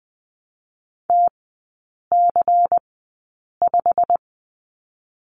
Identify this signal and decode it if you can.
Morse code, heard as a keyed tone, about 21 wpm, spaced as for 8 wpm, 690 Hz. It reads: TC5
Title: 145 Lesson 24e - Letter y - Three Characters 20wpm